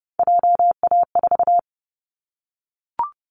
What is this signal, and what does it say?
Morse code, heard as a keyed tone, about 30 wpm, 700 Hz. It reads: JA4
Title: Callsign Prefixes for Top 50 DXCC - 30wpm